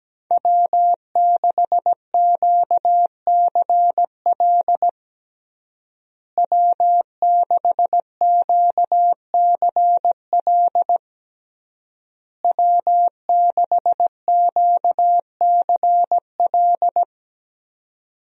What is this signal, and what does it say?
Morse code, heard as a keyed tone, about 17 wpm, 700 Hz. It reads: W6QCL W6QCL W6QCL
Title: USA Call Signs - 5 Characters - Repeated 3x - 17wpm